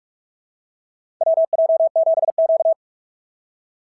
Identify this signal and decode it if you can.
Morse code, heard as a keyed tone, about 46 wpm, 690 Hz. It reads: WJ7Q
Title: SOTA QSO Practice 45wpm